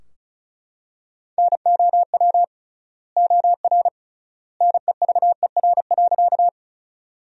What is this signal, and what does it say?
Morse code, heard as a keyed tone, about 36 wpm, 700 Hz. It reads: NOW OR NEVER.
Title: 3x Word-Spacing - Sentences from Top 300 Words - 35wpm